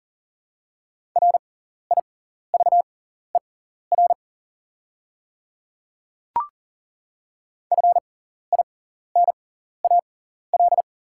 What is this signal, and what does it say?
Morse code, heard as a keyed tone, about 40 wpm, spaced as for 14 wpm, 690 Hz. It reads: RIVER FINAL
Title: 6x Character Spacing - Top 500 Words - 40wpm